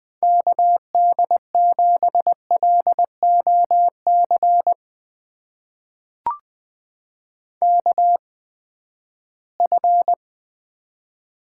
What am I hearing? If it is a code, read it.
Morse code, heard as a keyed tone, about 20 wpm, spaced as for 14 wpm, 700 Hz. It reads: KD7LOC KF
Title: USA Call Signs - 8x Character Spacing - 20wpm